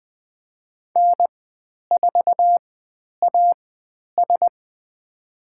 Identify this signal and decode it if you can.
Morse code, heard as a keyed tone, about 20 wpm, spaced as for 10 wpm, 700 Hz. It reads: N4AS
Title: 102 Lesson 17g - Letter u - Call Signs 20wpm